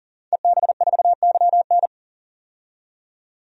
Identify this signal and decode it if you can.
Morse code, heard as a keyed tone, about 39 wpm, 700 Hz. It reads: EB4YN
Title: DXCC-100 Call Signs - 5 Characters - 40wpm